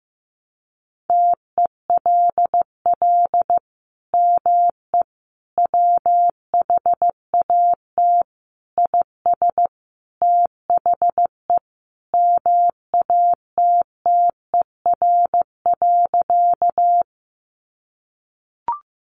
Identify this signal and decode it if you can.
Morse code, heard as a keyed tone, about 14 wpm, 700 Hz. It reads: TELL ME WHAT IS THE MATTER.
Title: Sentences from Top 700 Words 15wpm